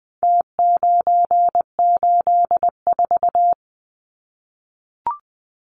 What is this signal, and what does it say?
Morse code, heard as a keyed tone, about 20 wpm, 700 Hz. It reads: T984
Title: Numbers - Contest Style - 4-Digit 20wpm